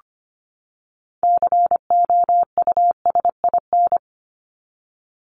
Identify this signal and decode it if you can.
Morse code, heard as a keyed tone, about 25 wpm, 700 Hz. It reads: COUSIN
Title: Top 1501-1750 Words 25wpm